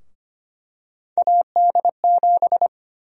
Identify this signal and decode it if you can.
Morse code, heard as a keyed tone, about 25 wpm, 700 Hz. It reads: AD7
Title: USA Call Sign Prefixes - Repeated 2x - 25wpm